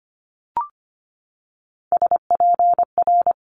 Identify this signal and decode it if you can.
Morse code, heard as a keyed tone, about 25 wpm, 700 Hz. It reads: SPR
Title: Multi-letter Phonemes 25wpm